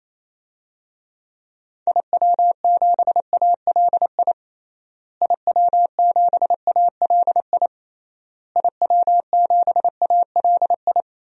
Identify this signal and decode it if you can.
Morse code, heard as a keyed tone, about 28 wpm, 700 Hz. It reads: IW7ALI IW7ALI IW7ALI
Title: DXCC-100 Call Signs - 6 Characters - Repeated 3x - 28wpm